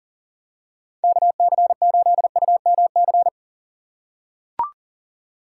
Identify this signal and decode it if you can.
Morse code, heard as a keyed tone, about 40 wpm, 700 Hz. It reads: KC8UMC
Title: USA Call Signs - 6 Characters - 4x Character Spacing - 40wpm